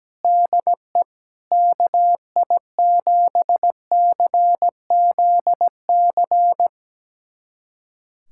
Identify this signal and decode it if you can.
Morse code, heard as a keyed tone, about 17 wpm, 700 Hz. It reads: DE KI7CZC
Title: USA Callsign DE Callsign - 6 Characters - 17wpm